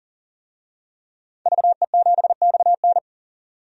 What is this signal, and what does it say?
Morse code, heard as a keyed tone, about 40 wpm, 700 Hz. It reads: VE7XN
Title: DXCC-100 Call Signs - 5 Characters - 40wpm